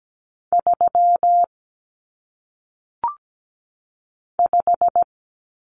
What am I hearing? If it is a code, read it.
Morse code, heard as a keyed tone, about 17 wpm, 700 Hz. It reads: 3 5
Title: Common Cut Numbers - 1-Digit 17wpm